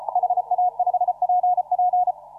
Morse code, heard as a keyed tone, about 33 wpm, 720 Hz. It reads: HA5PP
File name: CW / RHOB CW 2017 HA6P 0001